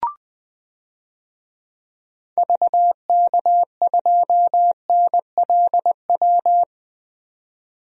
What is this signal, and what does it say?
Morse code, heard as a keyed tone, about 21 wpm, 700 Hz. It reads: VK2NLW
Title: DXCC-100 Call Signs - 6 Characters - 20wpm